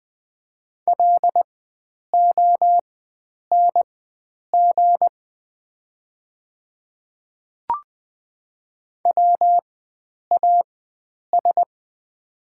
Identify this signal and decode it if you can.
Morse code, heard as a keyed tone, about 20 wpm, spaced as for 9 wpm, 690 Hz. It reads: LONG WAS
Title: 4x Character Spacing - Top 100 Words - 20wpm